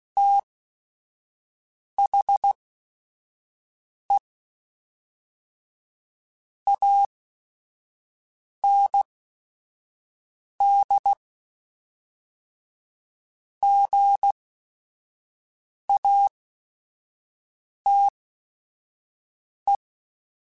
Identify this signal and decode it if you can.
Morse code, heard as a keyed tone, about 16 wpm, spaced as for 5 wpm, 800 Hz. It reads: THE AND GATE
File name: Technical Subjects / Technical Subjects-mp3s / AND Gate @5